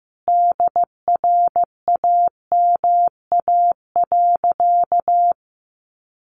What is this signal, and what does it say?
Morse code, heard as a keyed tone, about 15 wpm, 700 Hz. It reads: DRAMA.
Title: Sentences from Top 1500 Words 15wpm